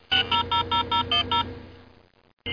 DTMF, heard as telephone touch tones, tones of about 120 ms. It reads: 6000010